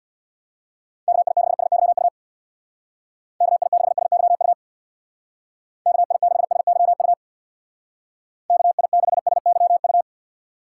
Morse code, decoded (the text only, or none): KI6SYU KI6SYU KI6SYU KI6SYU